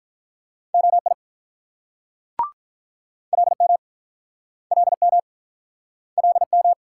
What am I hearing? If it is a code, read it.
Morse code, heard as a keyed tone, about 48 wpm, 700 Hz. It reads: OI LM LM LM